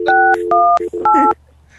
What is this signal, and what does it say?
DTMF, heard as telephone touch tones, tones of about 280 ms, with speech. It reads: 618